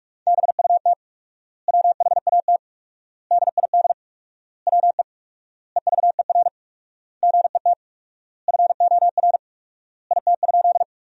Morse code, heard as a keyed tone, about 43 wpm, 700 Hz. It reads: BUT WHAT DID WE EVER GET FOR IT?
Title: 4x Word-Spacing - Sentences from Top 400 Words - 45wpm